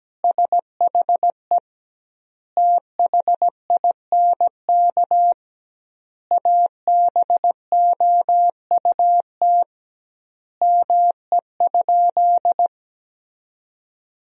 Morse code, heard as a keyed tone, about 17 wpm, 700 Hz. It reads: SHE THINK ABOUT ME?
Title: 2x Word-Spacing - Sentences from Top 200 Words - 17wpm